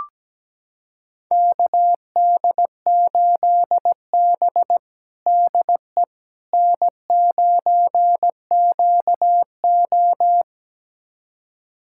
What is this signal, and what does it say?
Morse code, heard as a keyed tone, about 17 wpm, 700 Hz. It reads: KD8B DE N9QO
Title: USA Callsign DE Callsign - 4 Characters - 17wpm